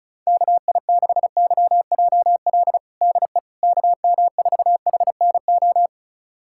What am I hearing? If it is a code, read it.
Morse code, heard as a keyed tone, about 36 wpm, 700 Hz. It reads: KI6YJL DE KM4HNO
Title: USA Callsign DE Callsign - 6 Characters - 35wpm